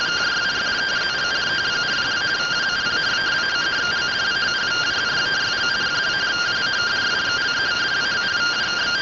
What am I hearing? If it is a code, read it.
Two-tone teleprinter signal, about 170 Hz shift.
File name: findings / AXIAL fr130 3mm 409.gen-11 rtty rx